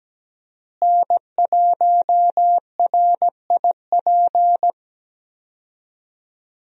Morse code, heard as a keyed tone, about 17 wpm, 700 Hz. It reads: N1RIP
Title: USA Call Signs - 5 Characters - Repeated 3x - 17wpm